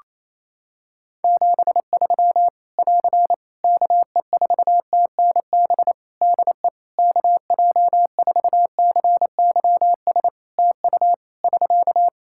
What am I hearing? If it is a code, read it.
Morse code, heard as a keyed tone, about 27 wpm, 700 Hz. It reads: 73 <AR> KE4TNB DE KJ4CYS TU <SK>
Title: Practice QSO - Exchange 4 - Ending - 28wpm